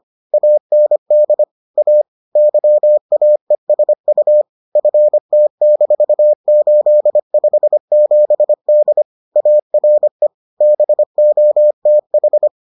Morse code, heard as a keyed tone, about 25 wpm, 600 Hz. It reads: AND A YAESU FT-857D ARE BOTH